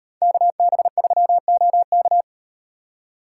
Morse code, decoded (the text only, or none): KB3OK